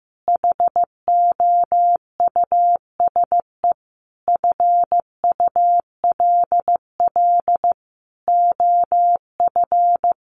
Morse code, heard as a keyed tone, about 15 wpm, 700 Hz. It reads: HOUSE FULL OF
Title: Sentences from Top 1500 Words 15wpm